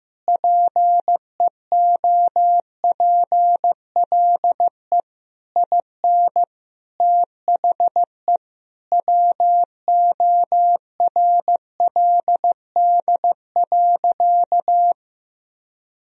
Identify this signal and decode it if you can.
Morse code, heard as a keyed tone, about 15 wpm, 700 Hz. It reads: PEOPLE IN THE WORLD.
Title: Sentences from Top 1000 Words - Review Part 4 - 15wpm